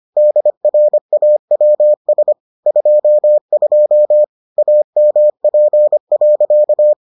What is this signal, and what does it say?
Morse code, heard as a keyed tone, about 25 wpm, 600 Hz. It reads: DRAWS 22 AMP.